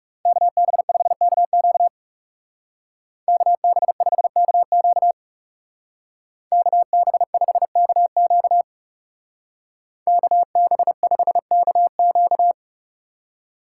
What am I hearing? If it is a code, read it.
Morse code, heard as a keyed tone, about 35 wpm, 700 Hz. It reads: KB5KQ KB5KQ KB5KQ KB5KQ
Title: Speed Racing - USA 2x2 Call Signs 30wpm